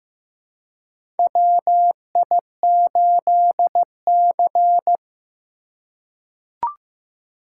Morse code, heard as a keyed tone, about 15 wpm, 700 Hz. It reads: WI8C
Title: USA Call Signs - 4 Characters - 2x Character Spacing - 15wpm